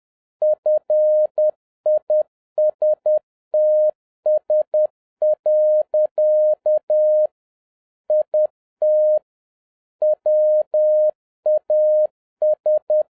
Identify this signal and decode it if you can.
Morse code, heard as a keyed tone, about 10 wpm, 610 Hz. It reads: FISTS. IT WAS